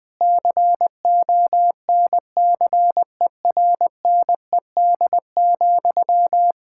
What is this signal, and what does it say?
Morse code, heard as a keyed tone, about 20 wpm, 700 Hz. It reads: CONCERNED,
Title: Sentences from Top 1000 Words - Review Part 6 - 20wpm